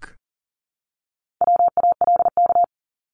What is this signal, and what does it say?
Morse code, heard as a keyed tone, about 40 wpm, 700 Hz. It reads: WALK
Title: Top 300 Words Repeated 40wpm